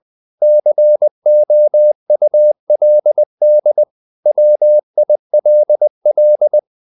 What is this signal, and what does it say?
Morse code, heard as a keyed tone, about 20 wpm, 600 Hz. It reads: COULD WILL
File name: CW / Practice / 500MostCommonEnglishWords20